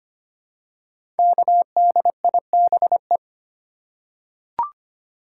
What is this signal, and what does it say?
Morse code, heard as a keyed tone, about 25 wpm, 700 Hz. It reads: KDIBE
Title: Code Groups - 5 Letters - 25wpm